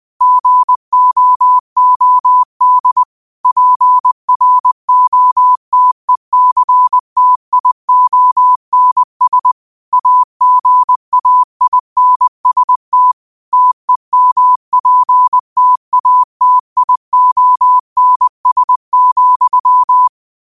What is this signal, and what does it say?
Morse code, heard as a keyed tone, about 20 wpm, 1000 Hz. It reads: GOOD PROTECTIONS AGAINST TEMPTATIONS,
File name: qotdfiles / 20wpm